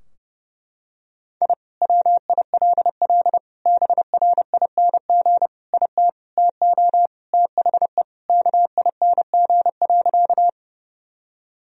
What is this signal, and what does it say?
Morse code, heard as a keyed tone, about 30 wpm, 700 Hz. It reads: I WILL BRING IT TO THE KING.
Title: Sentences from Top 1000 Words - Review Part 3 - 30wpm